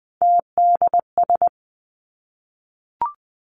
Morse code, heard as a keyed tone, about 20 wpm, 700 Hz. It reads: TDS
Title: Three Challenging Letter-Number Combinations 20wpm